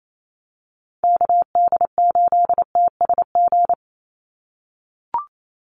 Morse code, heard as a keyed tone, about 28 wpm, 700 Hz. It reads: KD8TSG
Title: USA Call Signs 28wpm